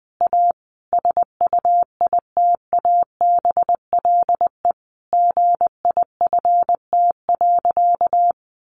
Morse code, heard as a keyed tone, about 20 wpm, 700 Hz. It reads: A SUITABLE GIFT.
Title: Sentences from Top 1400 Words 20wpm